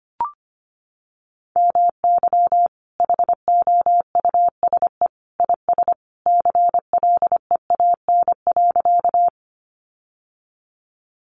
Morse code, heard as a keyed tone, about 25 wpm, 700 Hz. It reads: MY HOUSE IS CLEAN.
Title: Sentences from Top 1100 Words 25wpm